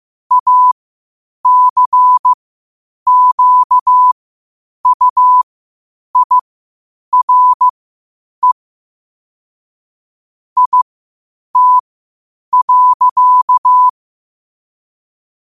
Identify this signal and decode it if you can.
Morse code, heard as a keyed tone, about 15 wpm, spaced as for 8 wpm, 1000 Hz. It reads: ACQUIRE IT.
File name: qotdfiles / 8wpm